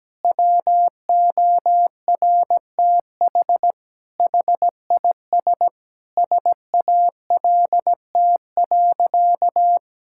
Morse code, heard as a keyed tone, about 17 wpm, 700 Hz. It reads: WORTH HIS SALT.
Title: Sentences from Top 1000 Words - Review Part 5 - 17wpm